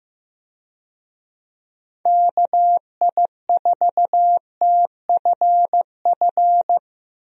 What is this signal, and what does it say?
Morse code, heard as a keyed tone, about 15 wpm, 700 Hz. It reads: KI4TFF
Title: USA Call Signs - 6 Characters - Repeated 3x - 15wpm